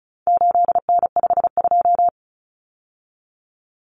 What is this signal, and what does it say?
Morse code, heard as a keyed tone, about 34 wpm, 700 Hz. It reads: 8N52
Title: Common Cut Numbers - 4-Digit 35wpm